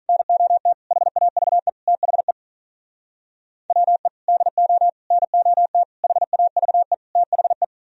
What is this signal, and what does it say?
Morse code, heard as a keyed tone, about 44 wpm, 700 Hz. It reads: NOT HAVE THE WE DO NOT HAVE THE